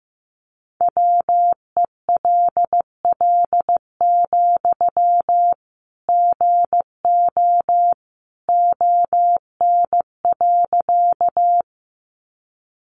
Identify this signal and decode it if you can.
Morse code, heard as a keyed tone, about 15 wpm, 700 Hz. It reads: WELL, GO ON.
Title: Sentences from Top 200 Words 15wpm